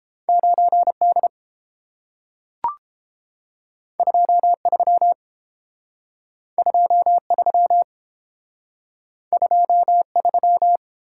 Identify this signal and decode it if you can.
Morse code, heard as a keyed tone, about 32 wpm, 700 Hz. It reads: 9D 23 23 23